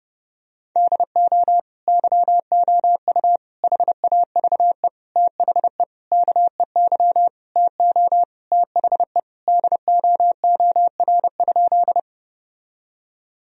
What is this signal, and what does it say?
Morse code, heard as a keyed tone, about 28 wpm, 700 Hz. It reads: DO YOU HAVE THE KEY TO THE DOOR?
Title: Sentences from Top 1000 Words - Review Part 4 - 30wpm